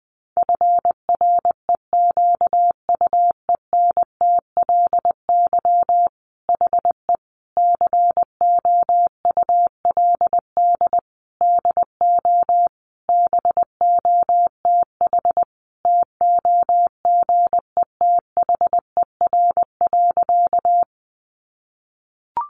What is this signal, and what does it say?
Morse code, heard as a keyed tone, about 20 wpm, 700 Hz. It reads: FREQUENTLY HE COULD DO BOTH TOGETHER.
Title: Sentences from Top 1300 Words 20wpm